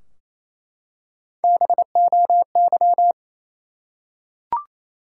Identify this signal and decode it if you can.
Morse code, heard as a keyed tone, about 28 wpm, 700 Hz. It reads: BOY
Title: Three Letter Words - 28wpm